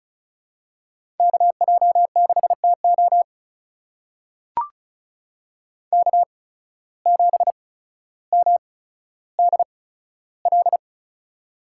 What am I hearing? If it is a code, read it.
Morse code, heard as a keyed tone, about 36 wpm, spaced as for 15 wpm, 700 Hz. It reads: KJ6TO K7MDL
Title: USA Call Signs - 5 Characters - 8x Character Spacing - 35wpm